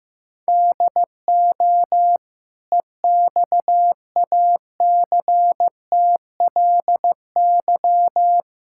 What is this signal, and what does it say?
Morse code, heard as a keyed tone, about 15 wpm, 700 Hz. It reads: DO EXACTLY
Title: Sentences from Top 1000 Words - Review Part 6 - 15wpm